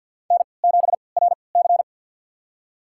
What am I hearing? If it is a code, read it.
Morse code, heard as a keyed tone, about 51 wpm, spaced as for 28 wpm, 700 Hz. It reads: N7RC
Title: USA Call Signs - 3x Character Spacing - 50wpm